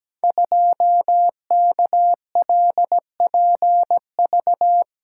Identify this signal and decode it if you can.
Morse code, heard as a keyed tone, about 17 wpm, 700 Hz. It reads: 2KLPV
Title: Code Groups - 5 Letter:Numbers - 17wpm